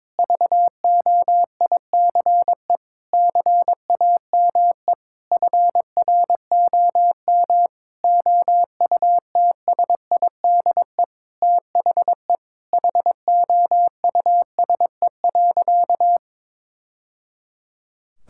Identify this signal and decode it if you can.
Morse code, heard as a keyed tone, about 22 wpm, 700 Hz. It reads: VOICE CAME FROM OUTSIDE THE HOUSE.
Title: Sentences from Top 1000 Words - Review Part 6 - 22wpm